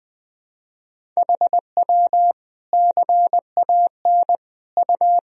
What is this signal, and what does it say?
Morse code, heard as a keyed tone, about 20 wpm, 700 Hz. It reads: HW CAN U
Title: Sentences from Top 100 Words and Abbreviations 20wpm